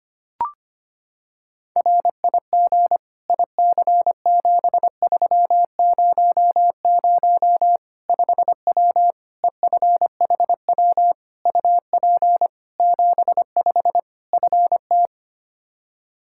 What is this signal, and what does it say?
Morse code, heard as a keyed tone, about 25 wpm, 700 Hz. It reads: RIG IC7300 5W EFHW UP 75 FT
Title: Practice QSO - Exchange 2 - Info - 25wpm